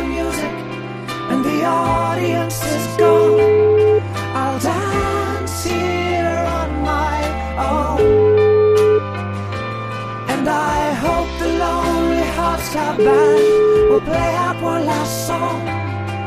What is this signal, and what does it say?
Ringback tone (Europe, 425 Hz), with noise.